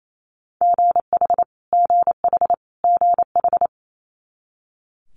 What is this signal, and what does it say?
Morse code, heard as a keyed tone, about 28 wpm, 700 Hz. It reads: GH GH GH
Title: Multi-letter Phonemes Repeated 3x 28wpm